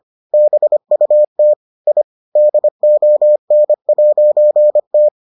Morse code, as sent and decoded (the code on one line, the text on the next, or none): -... ..- - / .. / -.. --- -. .----. -
BUT I DON'T